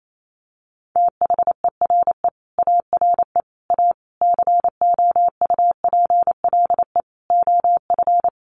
Morse code, heard as a keyed tone, about 28 wpm, 690 Hz. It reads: THERE ARE A COUPLE OF
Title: Sets of 5 Words Encore 28wpm